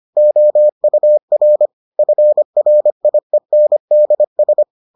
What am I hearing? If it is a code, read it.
Morse code, heard as a keyed tone, about 25 wpm, 600 Hz. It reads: OUR FRIENDS